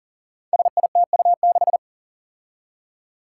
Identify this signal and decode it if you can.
Morse code, heard as a keyed tone, about 41 wpm, 700 Hz. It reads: SITU6
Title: Five Challenging Letter-Number Combinations 40wpm